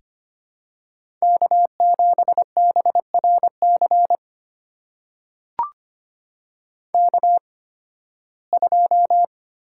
Morse code, heard as a keyed tone, about 25 wpm, spaced as for 16 wpm, 700 Hz. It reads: K7BRC K2
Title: USA Call Signs - 5 Characters - 8x Character Spacing - 25wpm